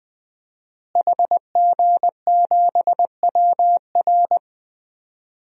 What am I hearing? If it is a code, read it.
Morse code, heard as a keyed tone, about 20 wpm, 700 Hz. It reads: HG7WR
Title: DXCC-100 Call Signs - 5 Characters - Repeated 2x - 20wpm